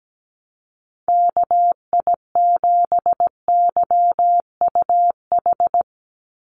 Morse code, heard as a keyed tone, about 17 wpm, 700 Hz. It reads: KI7YUH
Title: USA Call Signs 17wpm